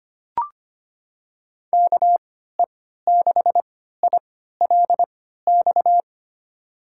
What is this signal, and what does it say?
Morse code, heard as a keyed tone, about 25 wpm, spaced as for 14 wpm, 700 Hz. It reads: KE6ILX
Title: USA Call Signs - 3x Character Spacing - 25wpm